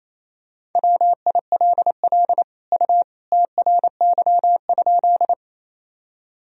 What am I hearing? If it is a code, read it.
Morse code, heard as a keyed tone, about 28 wpm, 700 Hz. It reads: WILL U TRY?
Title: Sentences from Top 200 Words and Abbreviations 28wpm